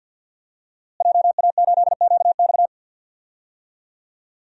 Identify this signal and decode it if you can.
Morse code, heard as a keyed tone, about 51 wpm, 700 Hz. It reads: JA8QX
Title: DXCC-100 Call Signs - 5 Characters - Repeated 2x - 50wpm